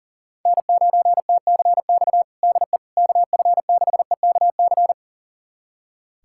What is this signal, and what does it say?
Morse code, heard as a keyed tone, about 40 wpm, 700 Hz. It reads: N9TCX DE KF6EKC
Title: USA Callsign DE Callsign 40wpm